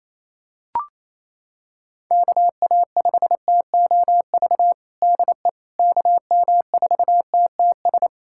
Morse code, heard as a keyed tone, about 28 wpm, 700 Hz. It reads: KA5TOV DE KM4TTS
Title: USA Callsign DE Callsign - 6 Characters - 28wpm